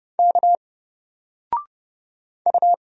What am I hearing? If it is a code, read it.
Morse code, heard as a keyed tone, about 30 wpm, 700 Hz. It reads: K U
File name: Single-Letter-Number-30wpm